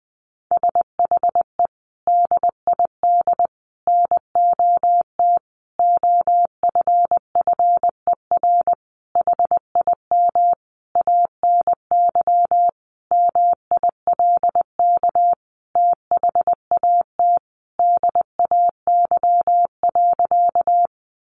Morse code, heard as a keyed tone, about 20 wpm, 700 Hz. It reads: SHE DID NOT OFFER HIM ANY MILK THAT DAY.
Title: Sentences from Top 1500 Words 20wpm